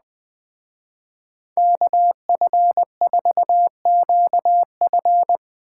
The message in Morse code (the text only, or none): KF4QF